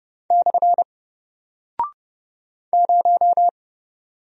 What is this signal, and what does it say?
Morse code, heard as a keyed tone, about 30 wpm, 700 Hz. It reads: / 0
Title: Single Character (Letter:Number:Punctuation:Prosign) Alt 30wpm